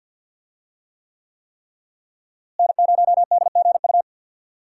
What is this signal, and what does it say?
Morse code, heard as a keyed tone, about 52 wpm, 700 Hz. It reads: N0DGU
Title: USA Call Signs - 5 Characters - Repeated 3x - 50wpm